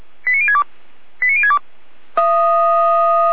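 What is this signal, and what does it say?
Five-tone selective call (ZVEI), heard as quick runs of five tones, tones of about 70 ms. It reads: 89062 89062
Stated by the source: ZVEI five-tone call 89125 received via SDR.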